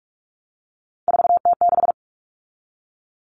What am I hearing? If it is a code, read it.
Morse code, heard as a keyed tone, about 44 wpm, 700 Hz. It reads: 4T6
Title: All Cut Numbers - 3-Digit 45wpm